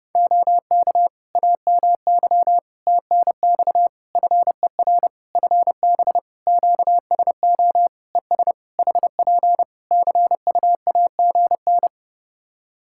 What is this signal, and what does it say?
Morse code, heard as a keyed tone, about 30 wpm, 700 Hz. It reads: OK AMY TNX FER FB QSO ES HP CUAGN
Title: Practice QSO - Exchange 4 - Ending - 30wpm